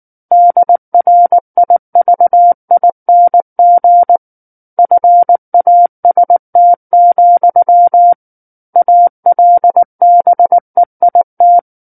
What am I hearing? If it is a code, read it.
Morse code, heard as a keyed tone, about 19 wpm, 700 Hz. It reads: DRIVING FAST, ALBEIT
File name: Book One / 10-Chapter 10 - 19 WPM